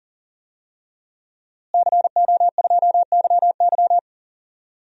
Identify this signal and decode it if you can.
Morse code, heard as a keyed tone, about 40 wpm, 700 Hz. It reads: CO2YY
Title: DXCC-100 Call Signs - 5 Characters - Repeated 3x - 40wpm